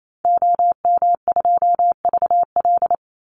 Morse code, heard as a keyed tone, about 28 wpm, 700 Hz. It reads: OM2VL
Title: Call Signs Repeated 3x - 28wpm